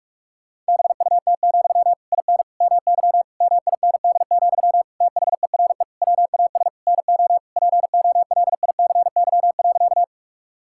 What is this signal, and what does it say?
Morse code, heard as a keyed tone, about 46 wpm, 700 Hz. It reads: BUT, IN MY MIND, THERE WAS NO POLICY.
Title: Sentences from Top 1000 Words - Review Part 5 - 45wpm